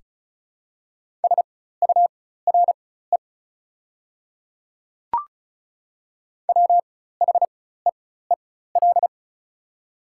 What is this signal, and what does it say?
Morse code, heard as a keyed tone, about 34 wpm, spaced as for 16 wpm, 690 Hz. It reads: SURE WHEEL
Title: 4x Character Spacing - Top 500 Words - 35wpm